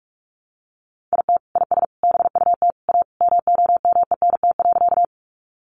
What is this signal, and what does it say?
Morse code, heard as a keyed tone, about 44 wpm, 700 Hz. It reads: IT IS BUT A MOMENT.
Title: Sentences from Top 1000 Words 45wpm